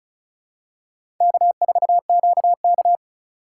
Morse code, heard as a keyed tone, about 36 wpm, 700 Hz. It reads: K4QK
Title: USA Call Signs - 4 Characters - 2x Character Spacing - 35wpm